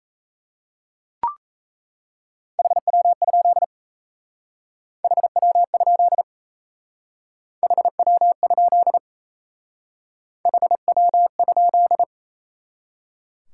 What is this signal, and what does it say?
Morse code, heard as a keyed tone, about 35 wpm, 700 Hz. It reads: HW? HW? HW? HW?
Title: Speed-Racing Top 180 QSO Elements 28wpm